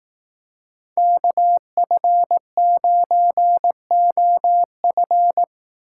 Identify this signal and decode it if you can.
Morse code, heard as a keyed tone, about 18 wpm, 700 Hz. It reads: KF9OF
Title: Speed Racing - USA 2x2 Call Signs 15wpm